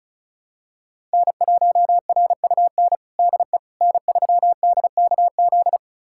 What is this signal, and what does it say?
Morse code, heard as a keyed tone, about 36 wpm, 690 Hz. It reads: N1RUN DE N3DKZ
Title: USA Callsign DE Callsign - 5 Characters - 35wpm